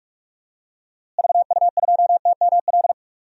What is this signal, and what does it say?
Morse code, heard as a keyed tone, about 44 wpm, 700 Hz. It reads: VU2TML
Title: DXCC-100 Call Signs - 6 Characters - Repeated 2x - 45wpm